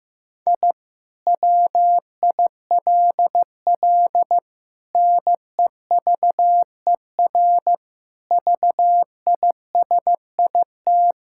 Morse code, heard as a keyed tone, about 16 wpm, 700 Hz. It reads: I WILL NEVER VISIT
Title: Sentences from Top 1000 Words - Review Part 6 - 15wpm